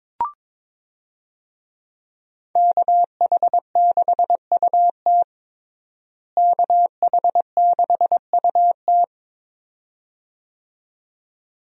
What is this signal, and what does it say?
Morse code, heard as a keyed tone, about 22 wpm, 700 Hz. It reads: KH6UT KH6UT
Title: USA Call Signs - 5 Characters - Repeated 2x - 22wpm